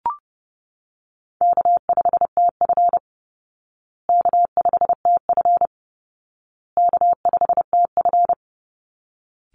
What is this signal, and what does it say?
Morse code, heard as a keyed tone, about 30 wpm, 700 Hz. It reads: K5TF K5TF K5TF
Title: Call Signs Repeated 2x - 30wpm